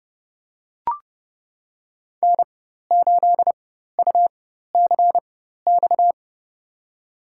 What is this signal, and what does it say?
Morse code, heard as a keyed tone, about 30 wpm, spaced as for 14 wpm, 700 Hz. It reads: N8UCX
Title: USA Call Signs - 5 Characters - 4x Character Spacing - 30wpm